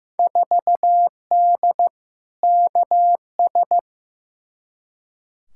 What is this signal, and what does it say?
Morse code, heard as a keyed tone, about 15 wpm, 700 Hz. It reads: 4D KS
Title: Field Day Exchanges Encore - 15wpm